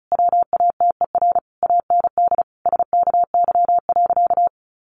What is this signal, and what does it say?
Morse code, heard as a keyed tone, about 34 wpm, 700 Hz. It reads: WATER AND SKY.
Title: Sentences from Top 1200 Words 35wpm